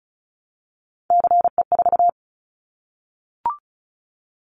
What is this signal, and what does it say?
Morse code, heard as a keyed tone, about 36 wpm, 700 Hz. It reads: CE4
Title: Callsign Prefixes for Top 100 DXCC - 35wpm